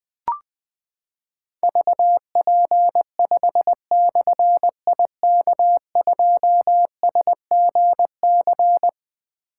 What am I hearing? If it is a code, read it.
Morse code, heard as a keyed tone, about 20 wpm, 700 Hz. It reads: VP5/IK2SGC
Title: Difficult Call Signs 20wpm